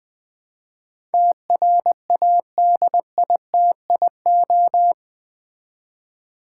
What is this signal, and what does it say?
Morse code, heard as a keyed tone, about 20 wpm, 700 Hz. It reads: TRADITIO
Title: Build A Word - 4 Letters At A Time - 11-Letter Words - 20wpm